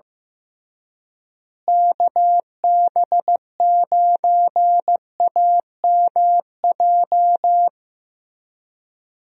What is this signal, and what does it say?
Morse code, heard as a keyed tone, about 15 wpm, 700 Hz. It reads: KB9AMJ
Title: USA Call Signs - 6 Characters - Repeated 3x - 15wpm